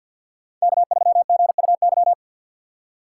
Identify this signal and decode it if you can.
Morse code, heard as a keyed tone, about 46 wpm, 700 Hz. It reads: K3GUY
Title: USA Call Signs - 5 Characters - 6x Character Spacing - 50wpm